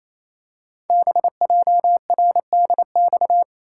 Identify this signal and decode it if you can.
Morse code, heard as a keyed tone, about 29 wpm, 700 Hz. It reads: BJRDX
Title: Code Groups - 5 Letters - 28wpm